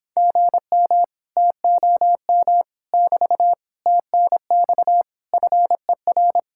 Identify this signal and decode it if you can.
Morse code, heard as a keyed tone, about 26 wpm, 700 Hz. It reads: GM TOM <BT> TNX FER